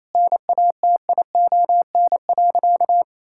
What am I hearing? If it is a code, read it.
Morse code, heard as a keyed tone, about 28 wpm, 700 Hz. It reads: NATION.
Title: Sentences from Top 1000 Words - Review Part 6 - 28wpm